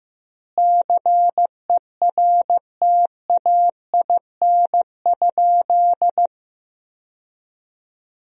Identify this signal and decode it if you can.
Morse code, heard as a keyed tone, about 15 wpm, 700 Hz. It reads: CERTAIN?
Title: 4x Word-Spacing - Sentences from Top 500 Words - 15wpm